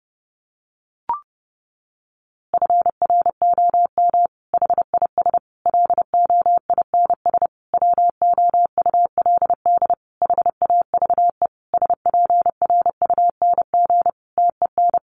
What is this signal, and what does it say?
Morse code, heard as a keyed tone, about 30 wpm, 700 Hz. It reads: FROM HIS LOINS WOULD HAVE SPRUNG TEN
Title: A Sound Of Thunder 30wpm